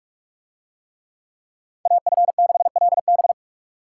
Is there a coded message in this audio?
Yes: Morse code from 1.9 to 3.3 s, heard as a keyed tone, about 40 wpm, 700 Hz. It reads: AF6LB